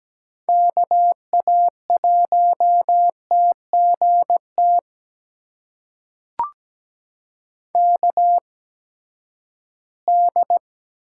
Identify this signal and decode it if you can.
Morse code, heard as a keyed tone, about 17 wpm, spaced as for 12 wpm, 700 Hz. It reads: KA1TGT KD
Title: USA Call Signs - 6 Characters - 8x Character Spacing - 17wpm